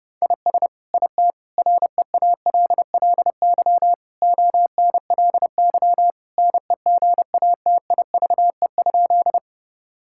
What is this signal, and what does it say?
Morse code, heard as a keyed tone, about 30 wpm, 700 Hz. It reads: IS IT REALLY ONLY NEGATIVE?
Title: Sentences from Top 1000 Words - Review Part 6 - 30wpm